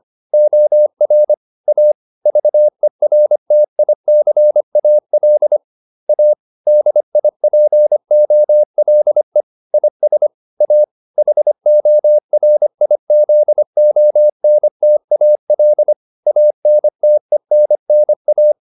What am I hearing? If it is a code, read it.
Morse code, heard as a keyed tone, about 25 wpm, 600 Hz. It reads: OR A VERTICAL A DIPOLE IS A HORIZONTAL ANTENNA
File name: SS202 25